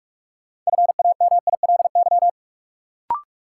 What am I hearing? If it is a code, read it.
Morse code, heard as a keyed tone, about 46 wpm, 700 Hz. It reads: FAMILY
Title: Sentences from Top 400 Words - New Word Drill - 45wpm